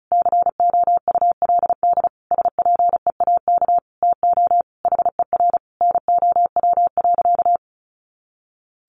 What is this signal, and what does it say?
Morse code, heard as a keyed tone, about 34 wpm, 700 Hz. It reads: COULD SPEAK TO HER NOW.
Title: Sentences from Top 1400 Words 35wpm